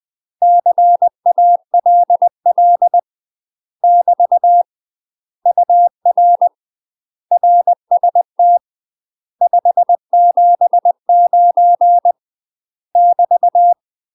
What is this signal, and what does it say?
Morse code, heard as a keyed tone, about 20 wpm, 700 Hz. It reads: CALL <BT> UR RST 579 <BT>
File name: UK MASTER QSOs TRAINING / 20WPM / UK MASTER QSOs 0045